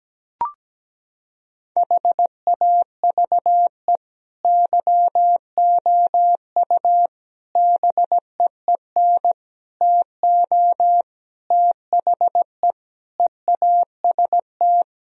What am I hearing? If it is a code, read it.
Morse code, heard as a keyed tone, about 17 wpm, 700 Hz. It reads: HAVE YOU BEEN TO THE EAST
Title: Sentences from Top 1000 Words - Review Part 3 - 17wpm